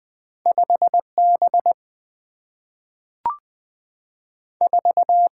Morse code, heard as a keyed tone, about 20 wpm, 700 Hz. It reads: 5 B 4
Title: Two Letter-Number Combinations - 20wpm